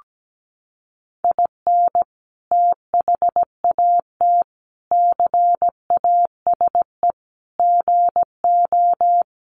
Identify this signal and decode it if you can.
Morse code, heard as a keyed tone, about 17 wpm, 700 Hz. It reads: IN THAT CASE GO
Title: Sentences from Top 700 Words 17wpm